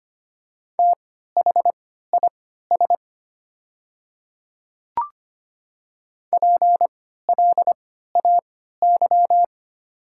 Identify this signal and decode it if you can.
Morse code, heard as a keyed tone, about 25 wpm, spaced as for 14 wpm, 700 Hz. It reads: THIS PLAY
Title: 3x Character Spacing - Top 200 Words - 25wpm